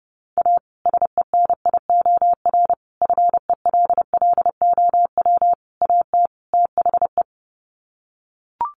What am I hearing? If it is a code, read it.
Morse code, heard as a keyed tone, about 30 wpm, 700 Hz. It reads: A SENIOR FELLOW AT THE
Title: Sets of 5 Words Encore 30wpm